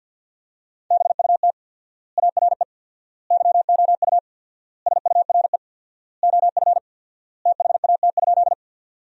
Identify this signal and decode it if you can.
Morse code, heard as a keyed tone, about 53 wpm, 700 Hz. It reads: BUT ARE YOU SURE OF THAT?
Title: 4x Word-Spacing - Sentences from Top 300 Words - 50wpm